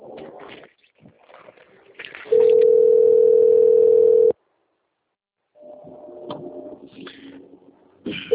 Ringback tone (North America), 440 + 480 Hz.